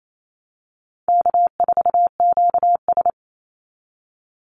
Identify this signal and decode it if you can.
Morse code, heard as a keyed tone, about 28 wpm, 700 Hz. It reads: K4QS
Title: NA SOTA Chaser Callsigns 28wpm